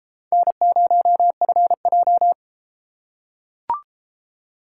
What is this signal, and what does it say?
Morse code, heard as a keyed tone, about 33 wpm, 700 Hz. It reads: N0FJ